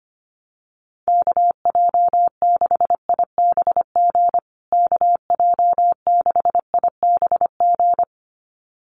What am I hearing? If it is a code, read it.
Morse code, heard as a keyed tone, about 25 wpm, 700 Hz. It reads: KJ6IBG KJ6IBG
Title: USA Call Signs Repeated 2x 25wpm